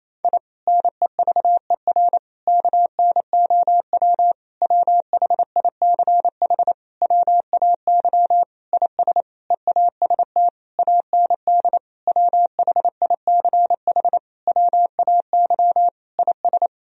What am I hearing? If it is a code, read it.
Morse code, heard as a keyed tone, about 28 wpm, 700 Hz. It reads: I NEVER KNOW WHICH WAY IS EAST AND WHICH WAY IS
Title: Sentences from Top 1000 Words - Review Part 3 - 28wpm